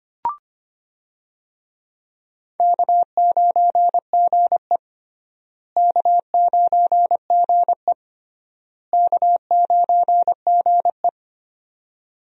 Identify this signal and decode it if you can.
Morse code, heard as a keyed tone, about 25 wpm, 700 Hz. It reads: K9GE K9GE K9GE
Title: USA Call Signs - 4 Characters - Repeated 3x - 25wpm